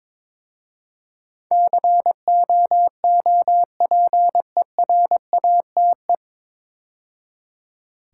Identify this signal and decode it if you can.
Morse code, heard as a keyed tone, about 22 wpm, 700 Hz. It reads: COOPERATE
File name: 9 Letter Words - First 3 Letters Given - 22wpm